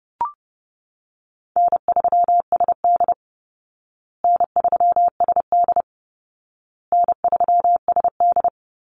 Morse code, heard as a keyed tone, about 30 wpm, 700 Hz. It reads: N3SD N3SD N3SD
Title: Easy Call Signs Repeated 3x - 30wpm